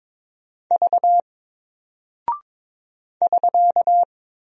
Morse code, heard as a keyed tone, about 21 wpm, 700 Hz. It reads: V <SK>
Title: Single Character (Letter:Number:Punctuation:Prosign) Alt 22wpm